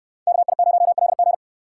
Morse code, heard as a keyed tone, about 65 wpm, 690 Hz. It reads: KI0LG